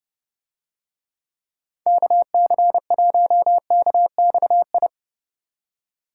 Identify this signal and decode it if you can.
Morse code, heard as a keyed tone, about 30 wpm, 700 Hz. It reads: KC1KXI